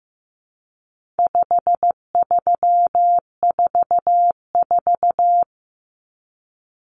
Morse code, heard as a keyed tone, about 15 wpm, 700 Hz. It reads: 5344
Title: Numbers - 4-Digit 15wpm